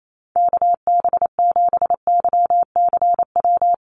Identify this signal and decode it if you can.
Morse code, heard as a keyed tone, about 28 wpm, 700 Hz. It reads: KB7YCW
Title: USA Call Signs Repeated 2x 28wpm